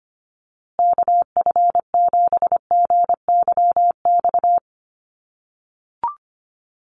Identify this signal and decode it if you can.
Morse code, heard as a keyed tone, about 25 wpm, 700 Hz. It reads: KF7GYX
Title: USA Call Signs Repeated 3x 25wpm